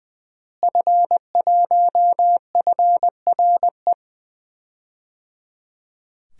Morse code, heard as a keyed tone, about 21 wpm, 700 Hz. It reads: F1FRE
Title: DXCC-100 Call Signs - 5 Characters - Repeated 3x - 20wpm